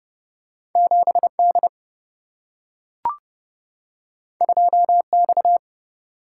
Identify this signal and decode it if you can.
Morse code, heard as a keyed tone, about 30 wpm, 700 Hz. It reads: 7 D 2 X
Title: Single-Letter Single-Number 30wpm